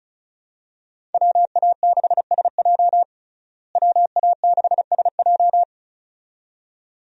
Morse code, heard as a keyed tone, about 34 wpm, 700 Hz. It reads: WA6SJ WA6SJ
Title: USA Call Signs - 5 Characters - Repeated 2x - 35wpm